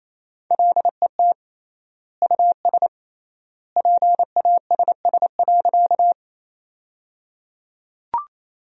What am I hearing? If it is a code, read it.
Morse code, heard as a keyed tone, about 28 wpm, 700 Hz. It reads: LET US PASS.
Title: 3x Word-Spacing - Sentences from Top 400 Words - 28wpm